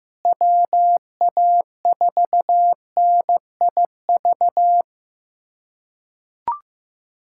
Morse code, heard as a keyed tone, about 16 wpm, 700 Hz. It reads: WA4NIV
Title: K1USN SST QSO Practice - 15wpm